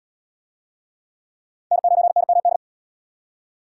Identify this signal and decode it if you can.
Morse code, heard as a keyed tone, about 76 wpm, 700 Hz. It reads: N9NMD